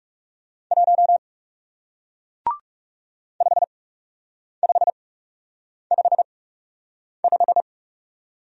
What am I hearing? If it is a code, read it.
Morse code, heard as a keyed tone, about 37 wpm, spaced as for 8 wpm, 700 Hz. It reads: 1 5555
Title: Speed-Racing Single Letter-Number 30wpm